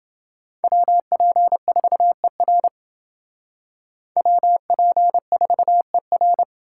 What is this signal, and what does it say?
Morse code, heard as a keyed tone, about 29 wpm, 700 Hz. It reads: WP4ER WP4ER